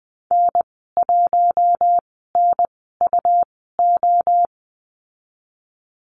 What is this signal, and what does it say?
Morse code, heard as a keyed tone, about 20 wpm, spaced as for 15 wpm, 700 Hz. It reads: N1NUO
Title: 154 Lesson 25g - Letter p - Call Signs 20wpm